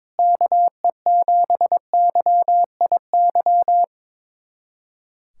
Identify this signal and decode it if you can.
Morse code, heard as a keyed tone, about 21 wpm, 700 Hz. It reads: KE7YIY
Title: USA Callsign DE Callsign - 6 Characters - 22wpm